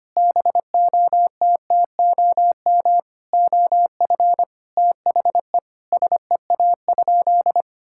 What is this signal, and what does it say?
Morse code, heard as a keyed tone, about 25 wpm, 700 Hz. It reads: BOTTOM OF THE SEA?
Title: Sentences from Top 1000 Words - Review Part 6 - 25wpm